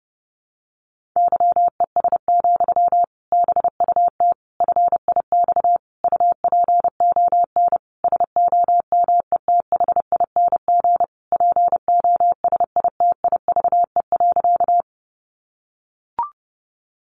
Morse code, heard as a keyed tone, about 30 wpm, 700 Hz. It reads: YES, BUT FIX UPON SOMETHING POSITIVE.
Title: Sentences from Top 1400 Words 30wpm